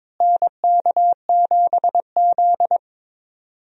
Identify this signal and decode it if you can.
Morse code, heard as a keyed tone, about 22 wpm, 700 Hz. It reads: NK7Z
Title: USA Call Signs - 4 Characters - Repeated 3x - 22wpm